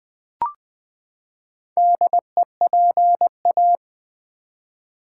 Morse code, heard as a keyed tone, about 20 wpm, 700 Hz. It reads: DEPA
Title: Build A Word - 4 Letters At A Time - 10-Letter Words - 20wpm